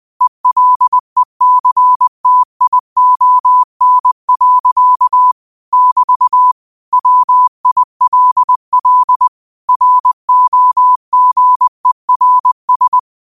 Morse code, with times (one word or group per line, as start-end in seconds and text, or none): 0.2-5.3 ELECTION.
5.7-6.5 <BT>
6.9-9.3 WILL
9.7-13.0 ROGERS